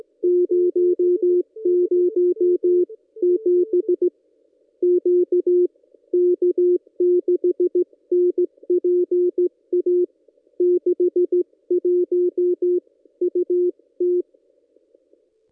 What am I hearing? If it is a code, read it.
Morse code, heard as a keyed tone, about 16 wpm, 360 Hz. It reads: 007 Q K6NPA 61 UT
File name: contests / ss-cw-k5tr-op-n5rz -1600